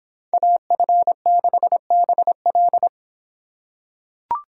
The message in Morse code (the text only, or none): AF6BL